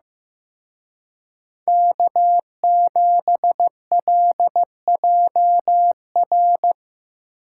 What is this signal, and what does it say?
Morse code, heard as a keyed tone, about 18 wpm, 700 Hz. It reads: K7LJR
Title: Speed Racing - USA 1x3 Call Signs 15wpm